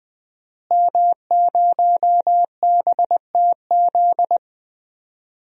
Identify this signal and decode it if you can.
Morse code, heard as a keyed tone, about 20 wpm, 700 Hz. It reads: M0BTZ
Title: DXCC-100 Call Signs - 5 Characters - Repeated 3x - 20wpm